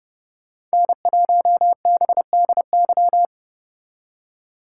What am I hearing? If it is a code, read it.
Morse code, heard as a keyed tone, about 34 wpm, 700 Hz. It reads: N1BDY